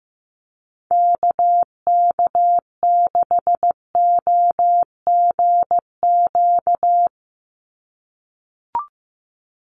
Morse code, heard as a keyed tone, about 15 wpm, 700 Hz. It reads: KK6OGQ
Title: USA Call Signs 15wpm